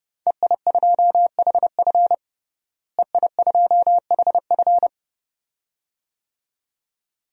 Morse code, heard as a keyed tone, about 30 wpm, 700 Hz. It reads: EI2HF EI2HF
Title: DXCC-100 Call Signs - 5 Characters - Repeated 2x - 30wpm